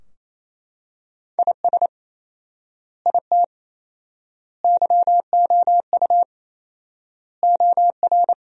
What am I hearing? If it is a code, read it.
Morse code, heard as a keyed tone, about 28 wpm, 700 Hz. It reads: IS IT YOU OR
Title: 4x Word-Spacing - Sentences from Top 200 Words - 28wpm